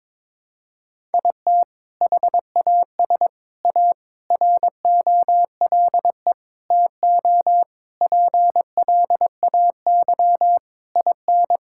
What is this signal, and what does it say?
Morse code, heard as a keyed tone, about 22 wpm, 700 Hz. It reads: IT HAS A ROLE TO PLAY IN
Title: Sentences from Top 1000 Words - Review Part 6 - 22wpm